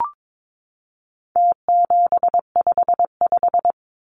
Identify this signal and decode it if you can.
Morse code, heard as a keyed tone, about 22 wpm, 700 Hz. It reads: T755
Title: Numbers - Contest Style - 4-Digit 22wpm